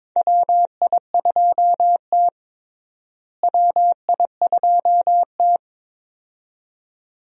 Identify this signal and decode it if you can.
Morse code, heard as a keyed tone, about 22 wpm, 700 Hz. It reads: WI2T WI2T
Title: USA Call Signs - 4 Characters - Repeated 2x - 22wpm